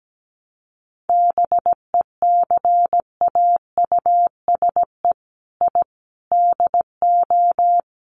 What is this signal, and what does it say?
Morse code, heard as a keyed tone, about 17 wpm, 700 Hz. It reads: BECAUSE I DO
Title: Sets of 4 Words 17wpm